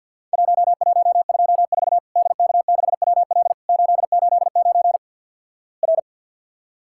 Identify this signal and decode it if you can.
Morse code, heard as a keyed tone, about 50 wpm, 700 Hz. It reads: 1124 DK6WL 889 R
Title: Worked All Europe DX Contest - QTC Exchange - 50wpm